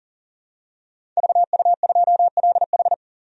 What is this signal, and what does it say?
Morse code, heard as a keyed tone, about 38 wpm, 700 Hz. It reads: VU2LH
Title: DXCC-100 Call Signs - 5 Characters - 40wpm